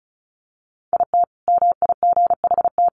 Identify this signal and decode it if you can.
Morse code, heard as a keyed tone, about 34 wpm, 700 Hz. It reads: IT MIGHT
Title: Sentences from Top 500 Words 35wpm